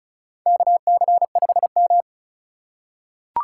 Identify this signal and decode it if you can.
Morse code, heard as a keyed tone, about 36 wpm, 700 Hz. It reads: KC5M
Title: USA Call Signs - 4 Characters - 2x Character Spacing - 35wpm